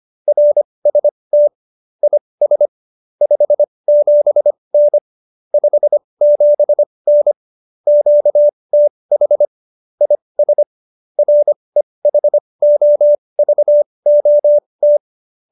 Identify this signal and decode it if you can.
Morse code, heard as a keyed tone, about 25 wpm, spaced as for 20 wpm, 590 Hz. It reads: RST IS 57N 57N QTH IS REHOVOT
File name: QSO310 20